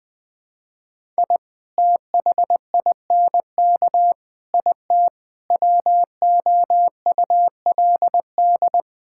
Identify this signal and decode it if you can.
Morse code, heard as a keyed tone, about 20 wpm, 700 Hz. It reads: I THINK IT WOULD
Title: Build A Phrase - 5-Word Phrases - 20wpm